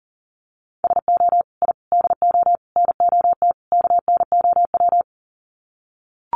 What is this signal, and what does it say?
Morse code, heard as a keyed tone, about 40 wpm, 700 Hz. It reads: SO I DO NOT KNOW
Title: Sets of 5 Words Encore 40wpm